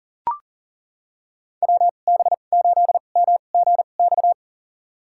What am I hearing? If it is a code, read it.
Morse code, heard as a keyed tone, about 40 wpm, spaced as for 29 wpm, 700 Hz. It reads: WB8MGX
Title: USA Call Signs - 6 Characters - 2x Character Spacing - 40wpm